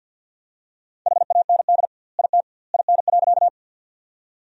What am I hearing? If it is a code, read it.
Morse code, heard as a keyed tone, about 50 wpm, 700 Hz. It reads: HAND IT IN.
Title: 2x Word-Spacing - Sentences from Top 200 Words - 50wpm